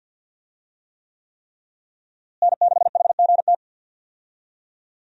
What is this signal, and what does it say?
Morse code, heard as a keyed tone, about 53 wpm, 700 Hz. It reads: N6HGT
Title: USA Call Signs - 5 Characters - Repeated 3x - 50wpm